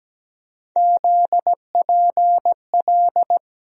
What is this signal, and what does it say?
Morse code, heard as a keyed tone, about 17 wpm, 700 Hz. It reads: ZPL
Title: USA Call Sign Postfixes - 3 Letters 17wpm